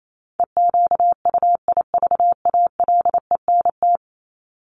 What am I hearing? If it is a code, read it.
Morse code, heard as a keyed tone, about 28 wpm, 700 Hz. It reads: EQUIVALENT
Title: Top 1501-1750 Words 28wpm